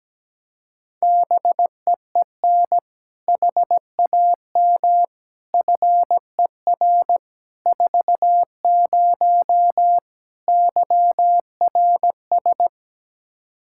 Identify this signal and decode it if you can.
Morse code, heard as a keyed tone, about 17 wpm, 700 Hz. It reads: BEEN HAM FER 40 YRS
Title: Practice QSO - Exchange 3 - More Info - 17wpm